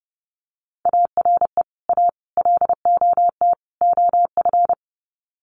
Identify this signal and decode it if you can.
Morse code, heard as a keyed tone, about 30 wpm, 700 Hz. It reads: ARE A LOT OF
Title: Sets of 4 Words 30wpm